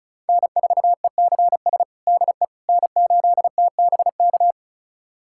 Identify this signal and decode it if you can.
Morse code, heard as a keyed tone, about 34 wpm, 690 Hz. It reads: N4ECS DE N8TBK
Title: USA Callsign DE Callsign - 5 Characters - 35wpm